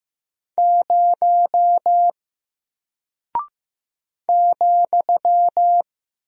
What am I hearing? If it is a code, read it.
Morse code, heard as a keyed tone, about 15 wpm, 700 Hz. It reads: 0 ,
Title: Single Character (Letter:Number:Punctuation:Prosign) Alt 15wpm